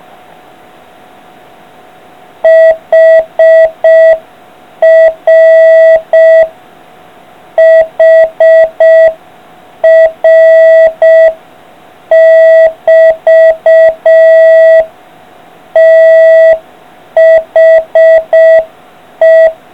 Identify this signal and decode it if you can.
Morse code, heard as a keyed tone, about 6 wpm, 650 Hz. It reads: HR HR<BT> THE